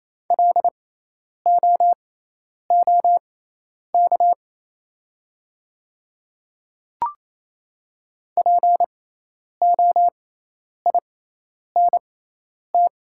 Morse code, heard as a keyed tone, about 28 wpm, spaced as for 10 wpm, 700 Hz. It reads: LOOK POINT
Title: 6x Character Spacing - Top 200 Words - 28wpm